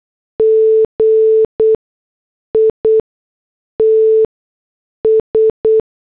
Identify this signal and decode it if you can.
Morse code, heard as a keyed tone, about 8 wpm, spaced as for 6 wpm, 440 Hz. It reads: GITS